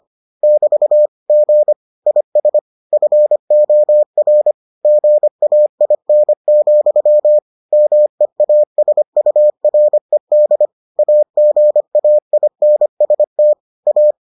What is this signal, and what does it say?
Morse code, heard as a keyed tone, about 25 wpm, 600 Hz. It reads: <BT> G IS FOR GAIN, MEASURED AGAINST A